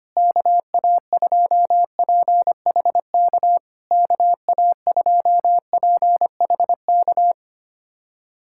Morse code, heard as a keyed tone, about 25 wpm, 690 Hz. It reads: KA2PHK KA2PHK
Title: SOTA QSO Practice 25wpm